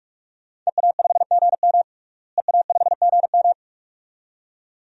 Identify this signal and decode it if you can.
Morse code, heard as a keyed tone, about 42 wpm, 700 Hz. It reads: EA5GM EA5GM
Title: DXCC-100 Call Signs - 5 Characters - Repeated 2x - 45wpm